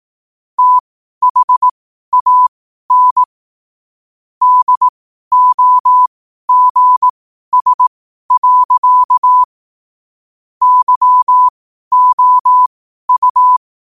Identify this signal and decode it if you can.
Morse code, heard as a keyed tone, about 18 wpm, spaced as for 12 wpm, 1000 Hz. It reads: THAN DOGS. YOU
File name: qotdfiles / 12wpm